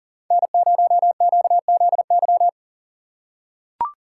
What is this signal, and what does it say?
Morse code, heard as a keyed tone, about 40 wpm, 700 Hz. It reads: N0QZY
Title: USA Call Signs - 6x Character Spacing - 40wpm